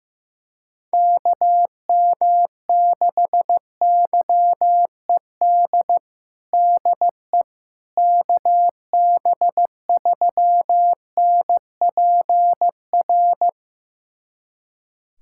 Morse code, heard as a keyed tone, about 15 wpm, 700 Hz. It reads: KM6YED DE KB3NPR
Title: USA Callsign DE Callsign - 6 Characters - 15wpm